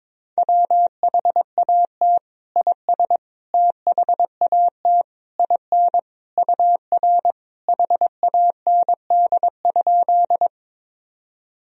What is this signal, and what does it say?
Morse code, heard as a keyed tone, about 22 wpm, 700 Hz. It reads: WHAT IS THAT IN UR HAND?
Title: Sentences from Top 200 Words and Abbreviations 22wpm